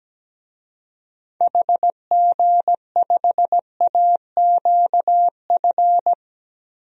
Morse code, heard as a keyed tone, about 17 wpm, 700 Hz. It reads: HG5AQF
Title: DXCC-100 Call Signs - 6 Characters - Repeated 3x - 17wpm